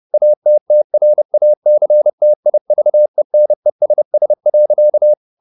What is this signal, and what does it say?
Morse code, heard as a keyed tone, about 30 wpm, 600 Hz. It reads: ATTRACTIVENESS.
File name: the old man and the sea / the old man and the sea 1 30wpm